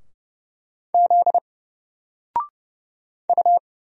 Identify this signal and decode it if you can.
Morse code, heard as a keyed tone, about 30 wpm, 700 Hz. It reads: Z U
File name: Single-Letter-Number-30wpm